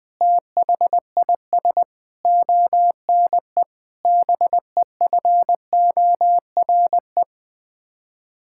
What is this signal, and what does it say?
Morse code, heard as a keyed tone, about 20 wpm, 700 Hz. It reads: THIS ONE BEFORE
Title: Sets of 8 Words - 20wpm